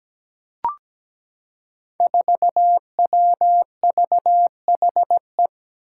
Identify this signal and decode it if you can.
Morse code, heard as a keyed tone, about 17 wpm, 700 Hz. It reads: 4WVHE
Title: Five Challenging Letter-Number Combinations 17wpm